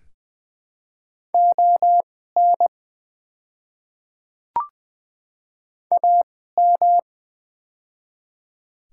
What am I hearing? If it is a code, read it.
Morse code, heard as a keyed tone, about 20 wpm, spaced as for 14 wpm, 700 Hz. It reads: ON AM
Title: 2x Character Spacing - Top 500 Words - 20wpm